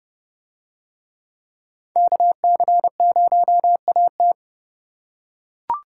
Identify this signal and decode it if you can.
Morse code, heard as a keyed tone, about 29 wpm, 700 Hz. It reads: KC0AT